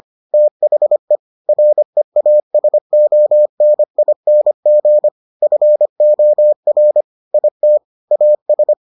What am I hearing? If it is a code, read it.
Morse code, heard as a keyed tone, about 25 wpm, 600 Hz. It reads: THE REASONING FOR IT AS